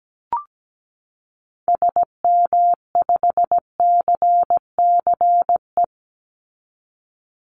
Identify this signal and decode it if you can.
Morse code, heard as a keyed tone, about 17 wpm, 700 Hz. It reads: SM5CCE
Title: Call Signs Repeated 3x - 17wpm